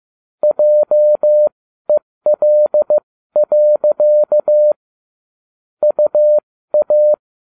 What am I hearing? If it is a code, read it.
Morse code, heard as a keyed tone, about 15 wpm, spaced as for 13 wpm, 600 Hz. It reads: JEL. UA